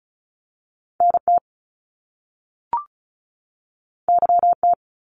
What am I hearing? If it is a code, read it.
Morse code, heard as a keyed tone, about 36 wpm, 700 Hz. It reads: NT YT
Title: Canadian Provinces and Territories 35wpm